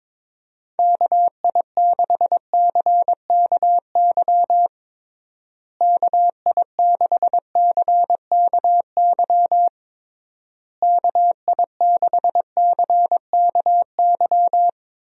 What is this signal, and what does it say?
Morse code, heard as a keyed tone, about 22 wpm, 700 Hz. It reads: KI6CKY KI6CKY KI6CKY
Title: USA Call Signs - 6 Characters - Repeated 3x - 22wpm